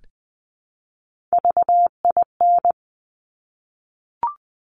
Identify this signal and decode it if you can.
Morse code, heard as a keyed tone, about 19 wpm, 700 Hz. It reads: VIN
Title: Top Three Letter Combinations in Words 20wpm